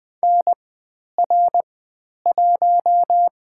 Morse code, heard as a keyed tone, about 20 wpm, 700 Hz. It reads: N R 1
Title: 102 Lesson 17g - Letter u - Call Signs 20wpm